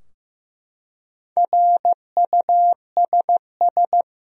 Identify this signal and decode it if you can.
Morse code, heard as a keyed tone, about 15 wpm, 700 Hz. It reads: RUSS
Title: USA Common Names - Max 4 Letters - 15wpm